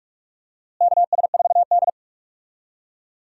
Morse code, heard as a keyed tone, about 46 wpm, 700 Hz. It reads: KS4D
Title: USA Call Signs - 4 Characters - Repeated 3x - 45wpm